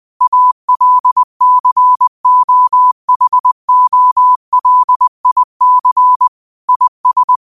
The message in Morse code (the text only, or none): ALCOHOLIC IS